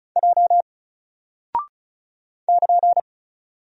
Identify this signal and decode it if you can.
Morse code, heard as a keyed tone, about 36 wpm, 700 Hz. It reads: J <KN>
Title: Single Character (Letter:Number:Punctuation:Prosign) 35wpm